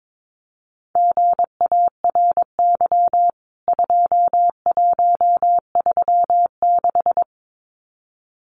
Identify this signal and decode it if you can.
Morse code, heard as a keyed tone, about 22 wpm, 700 Hz. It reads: GARY 2136
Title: CWT Exchange 22wpm